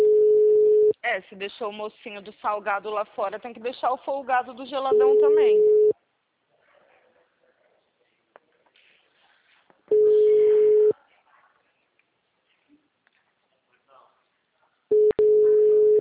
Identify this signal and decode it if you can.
Ringback tone (Europe, 425 Hz), with speech.
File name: out- -2019- - - .80199